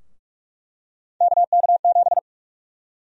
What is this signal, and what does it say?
Morse code, heard as a keyed tone, about 46 wpm, 700 Hz. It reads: KK7
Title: USA Call Sign Prefixes - 45wpm